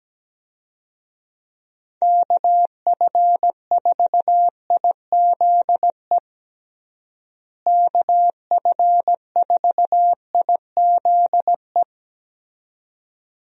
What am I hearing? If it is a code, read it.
Morse code, heard as a keyed tone, about 17 wpm, 700 Hz. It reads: KF4IZE KF4IZE
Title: USA Call Signs - 6 Characters - Repeated 2x - 17wpm